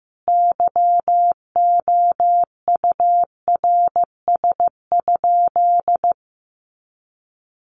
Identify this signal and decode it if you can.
Morse code, heard as a keyed tone, about 14 wpm, 700 Hz. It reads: YOURS?
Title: Sentences from Top 1500 Words 15wpm